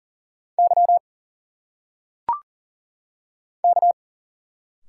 Morse code, heard as a keyed tone, about 40 wpm, 700 Hz. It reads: Y K
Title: Single Letters - Phonetic Pronunciation 40wpm